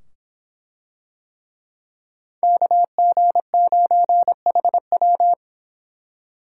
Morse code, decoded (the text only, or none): KG9HW